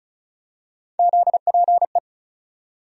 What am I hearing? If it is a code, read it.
Morse code, heard as a keyed tone, about 36 wpm, 700 Hz. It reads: ZPE
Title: USA Call Sign Postfixes - 3 Letters 35wpm